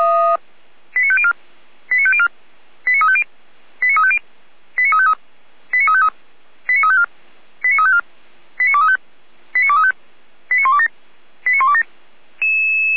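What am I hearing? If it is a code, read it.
Five-tone selective call (ZVEI), heard as quick runs of five tones, tones of about 70 ms. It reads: 89504 89504 89350 89350 89353 89353 89354 89354 89235 89235 89127 89127
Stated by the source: ZVEI five-tone call received by SDR, logged by Feuerwehr Orlamünde (f85360).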